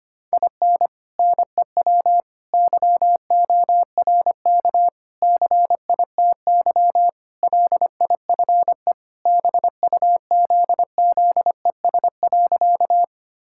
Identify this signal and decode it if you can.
Morse code, heard as a keyed tone, about 25 wpm, 700 Hz. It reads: IN NEW YORK CITY LIFE BUZZES.
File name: practice / 25wpm / Sentences with 4-Word Phrases - 25wpm